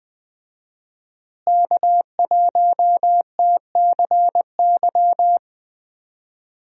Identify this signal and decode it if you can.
Morse code, heard as a keyed tone, about 20 wpm, 700 Hz. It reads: K1TCY
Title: USA Call Signs - 5 Characters - Repeated 3x - 20wpm